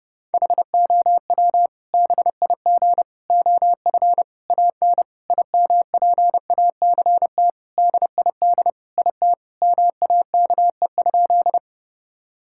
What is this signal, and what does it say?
Morse code, heard as a keyed tone, about 30 wpm, 700 Hz. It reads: HOW BIG OF AN IMPACT DID IT MAKE?
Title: Sentences from Top 1000 Words - Review Part 6 - 30wpm